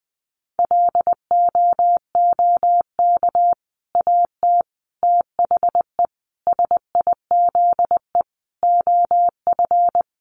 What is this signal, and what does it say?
Morse code, heard as a keyed tone, about 20 wpm, 700 Hz. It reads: LOOK AT THE SIZE OF
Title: Sentences from Top 1200 Words 20wpm